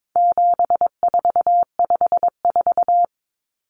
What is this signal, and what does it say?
Morse code, heard as a keyed tone, about 22 wpm, 700 Hz. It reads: 7454
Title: Numbers - 4-Digit 22wpm